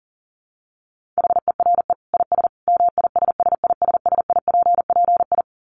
Morse code, heard as a keyed tone, about 40 wpm, 700 Hz. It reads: HERE IS MISSISSIPPI
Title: Dit Challenge 40wpm